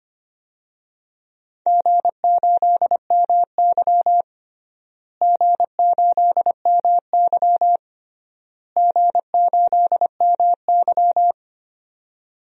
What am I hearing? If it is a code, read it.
Morse code, heard as a keyed tone, about 25 wpm, 700 Hz. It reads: G8MY G8MY G8MY
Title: DXCC-100 Call Signs - 4 Characters - Repeated 3x - 25wpm